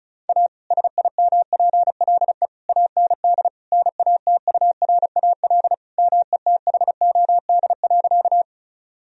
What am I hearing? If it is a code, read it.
Morse code, heard as a keyed tone, about 34 wpm, 700 Hz. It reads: A SIMPLE AND NATURAL METHOD.
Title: Sentences from Top 1000 Words - Review Part 4 - 35wpm